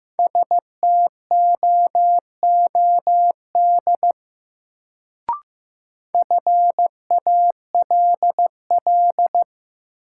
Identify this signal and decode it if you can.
Morse code, heard as a keyed tone, about 15 wpm, 700 Hz. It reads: STOOD FALL
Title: Sentences from Top 500 Words - New Word Drill - 15wpm